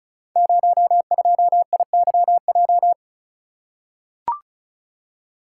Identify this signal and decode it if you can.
Morse code, heard as a keyed tone, about 36 wpm, 700 Hz. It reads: 02IYJ
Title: Code Groups - 5 Letter:Numbers - 35wpm